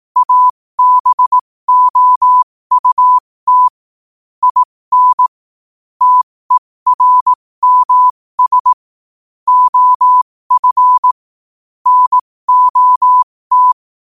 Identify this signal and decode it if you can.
Morse code, heard as a keyed tone, about 18 wpm, spaced as for 15 wpm, 1000 Hz. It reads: ABOUT IN TERMS OF NOT
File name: qotdfiles / 15wpm